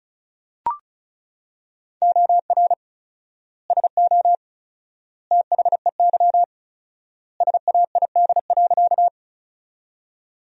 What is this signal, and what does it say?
Morse code, heard as a keyed tone, about 34 wpm, 700 Hz. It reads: OR SO THEY SAID.
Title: 4x Word-Spacing - Sentences from Top 100 Words - 35wpm